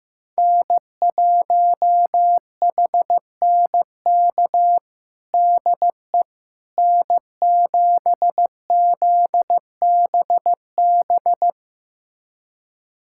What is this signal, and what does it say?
Morse code, heard as a keyed tone, about 16 wpm, 700 Hz. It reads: N1HNK DE N7ZBB
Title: USA Callsign DE Callsign - 5 Characters - 15wpm